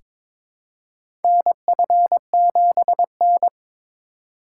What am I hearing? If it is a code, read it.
Morse code, heard as a keyed tone, about 22 wpm, 700 Hz. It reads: NF7N
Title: USA Call Signs - 4 Characters - 3x Character Spacing - 22wpm